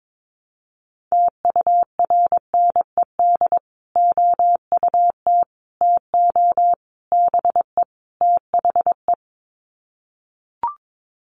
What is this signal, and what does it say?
Morse code, heard as a keyed tone, about 22 wpm, 700 Hz. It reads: TURNED OUT TO BE THE
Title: Sets of 5 Words Encore 22wpm